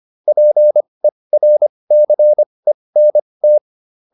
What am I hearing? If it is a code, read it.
Morse code, heard as a keyed tone, about 25 wpm, spaced as for 20 wpm, 600 Hz. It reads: PERCENT